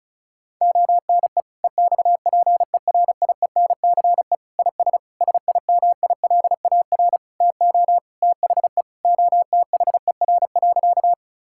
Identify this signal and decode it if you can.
Morse code, heard as a keyed tone, about 36 wpm, 700 Hz. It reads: ONE EXPERIENCE IS SIMILAR TO THE OTHER.
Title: Sentences from Top 1000 Words - Review Part 5 - 35wpm